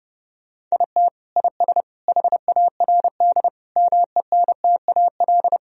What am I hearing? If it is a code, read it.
Morse code, heard as a keyed tone, about 32 wpm, 700 Hz. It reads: IT IS HARD MENTAL
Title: Sentences from Top 1000 Words - Review Part 4 - 30wpm